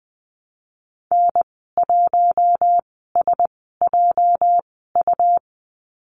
Morse code, heard as a keyed tone, about 20 wpm, spaced as for 14 wpm, 700 Hz. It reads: N1SJU
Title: 213 Lesson 34g - Number 8 - Call Signs 20wpm